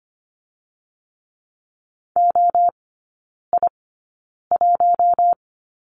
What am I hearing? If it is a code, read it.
Morse code, heard as a keyed tone, about 25 wpm, spaced as for 9 wpm, 700 Hz. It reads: OI1
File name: Lesson 15e - Number 5 - Three Characters 25wpm